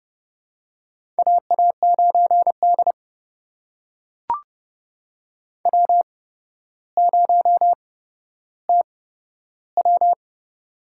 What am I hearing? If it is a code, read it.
Morse code, heard as a keyed tone, about 30 wpm, spaced as for 13 wpm, 700 Hz. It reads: AA9D W0TW
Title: USA Call Signs - 4 Characters - 8x Character Spacing - 30wpm